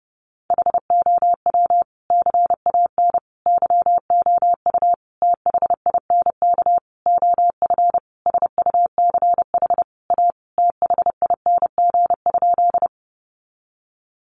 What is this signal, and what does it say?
Morse code, heard as a keyed tone, about 30 wpm, 700 Hz. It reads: HOW CAN YOU THINK OF SUCH A THING?
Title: Sentences from Top 200 Words 30wpm